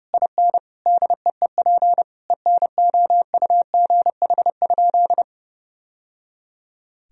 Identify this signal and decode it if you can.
Morse code, heard as a keyed tone, about 32 wpm, 700 Hz. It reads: IN DEEP ENOUGH?
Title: Sentences from Top 1000 Words - Review Part 3 - 30wpm